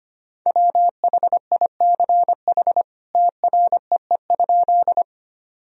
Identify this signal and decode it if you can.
Morse code, heard as a keyed tone, about 25 wpm, 700 Hz. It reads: WHICH TREE?
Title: Sentences from Top 1000 Words - Review Part 1 - 25wpm